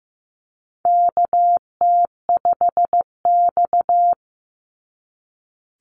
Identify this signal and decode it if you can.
Morse code, heard as a keyed tone, about 15 wpm, 700 Hz. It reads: KT5X
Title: NA SOTA Chaser Callsigns 15wpm